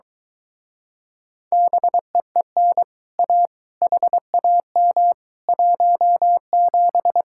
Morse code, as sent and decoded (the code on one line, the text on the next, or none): -... . -. / .- / .... .- -- / .---- --...
BEEN A HAM 17